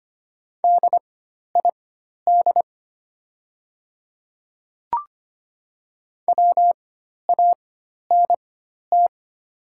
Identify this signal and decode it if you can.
Morse code, heard as a keyed tone, about 25 wpm, spaced as for 12 wpm, 700 Hz. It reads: DID WANT
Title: 4x Character Spacing - Top 200 Words - 25wpm